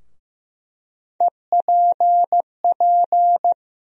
Morse code, heard as a keyed tone, about 15 wpm, 700 Hz. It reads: EPP
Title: USA Call Sign Postfixes - 3 Letters 15wpm